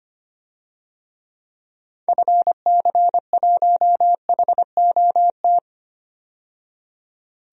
Morse code, heard as a keyed tone, about 25 wpm, 700 Hz. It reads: FC1HOT
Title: DXCC-100 Call Signs - 6 Characters - 25wpm